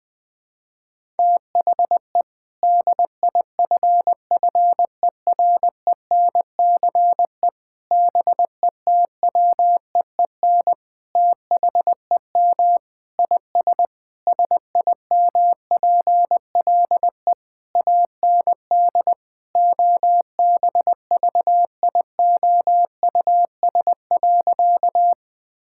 Morse code, heard as a keyed tone, about 20 wpm, 700 Hz. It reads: THE DIFFERENCE BETWEEN THEM IS SIMPLE AND OBVIOUS.
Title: Sentences from Top 1600 Words 20wpm